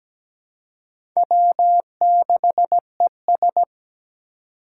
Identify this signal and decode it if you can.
Morse code, heard as a keyed tone, about 17 wpm, 700 Hz. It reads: W6ES
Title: Four Challenging Letter-Number Combinations 17wpm